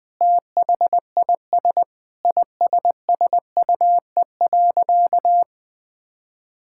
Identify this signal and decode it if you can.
Morse code, heard as a keyed tone, about 20 wpm, 700 Hz. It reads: THIS ISSUE.
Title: Sentences with 7-Word Phrases - Spoken Beginning - 20wpm